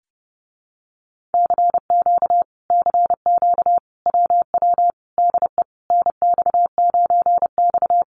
Morse code, heard as a keyed tone, about 30 wpm, 700 Hz. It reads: CQ CQ WW DE NX9X
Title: Common QSO Phrases Encore 30wpm